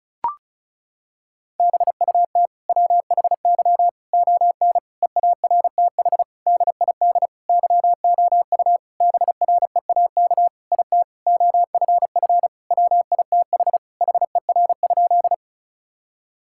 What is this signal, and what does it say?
Morse code, heard as a keyed tone, about 34 wpm, 700 Hz. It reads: BUT WHY ON EARTH DID YOU BREAK IT OFF WITH HER?
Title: Sentences from Top 1000 Words - Review Part 6 - 35wpm